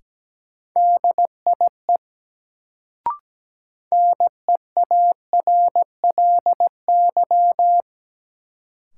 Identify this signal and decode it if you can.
Morse code, heard as a keyed tone, about 17 wpm, 700 Hz. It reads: DIE NEARLY
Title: Sentences from Top 1300 Words - New Word Drill Alt - 17wpm